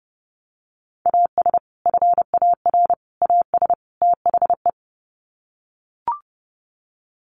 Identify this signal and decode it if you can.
Morse code, heard as a keyed tone, about 30 wpm, 700 Hz. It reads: AS FAR AS THE
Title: Sets of 4 Words 30wpm